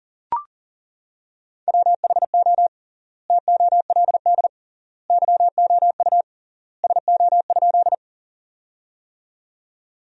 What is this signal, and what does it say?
Morse code, heard as a keyed tone, about 40 wpm, 700 Hz. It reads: WHO TOLD YOU SO?
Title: 3x Word-Spacing - Sentences from Top 400 Words - 40wpm